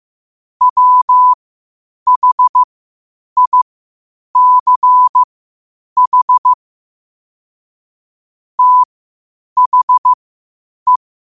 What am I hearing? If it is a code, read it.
Morse code, heard as a keyed tone, about 15 wpm, spaced as for 8 wpm, 1000 Hz. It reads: WHICH THE
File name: qotdfiles / 8wpm